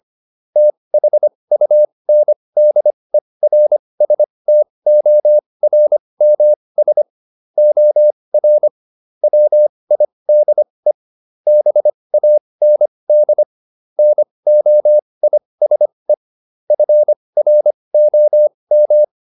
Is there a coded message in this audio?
Yes: Morse code from 0.6 to 19.0 s, heard as a keyed tone, about 25 wpm, spaced as for 20 wpm, 600 Hz. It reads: THUNDERSTORMS OR WIDE BAND NOISE FROM